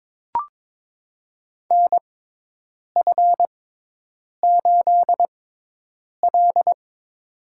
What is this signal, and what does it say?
Morse code, heard as a keyed tone, about 22 wpm, spaced as for 8 wpm, 700 Hz. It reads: NF8L
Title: USA Call Signs - 4 Characters - 6x Character Spacing - 22wpm